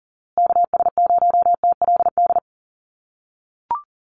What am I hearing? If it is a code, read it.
Morse code, heard as a keyed tone, about 40 wpm, 700 Hz. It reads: KS0TLD
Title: USA Call Signs Repeated 3x 40wpm